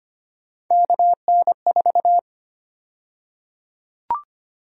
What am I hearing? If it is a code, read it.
Morse code, heard as a keyed tone, about 25 wpm, 700 Hz. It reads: KN4
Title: USA Call Sign Prefixes - 25wpm